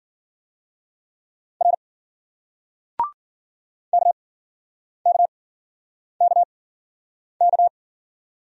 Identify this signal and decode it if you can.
Morse code, heard as a keyed tone, about 51 wpm, spaced as for 9 wpm, 700 Hz. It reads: R KKKK